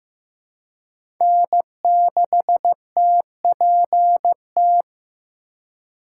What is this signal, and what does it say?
Morse code, heard as a keyed tone, about 15 wpm, 700 Hz. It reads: N6TPT
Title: USA Call Signs - 5 Characters - 4x Character Spacing - 15wpm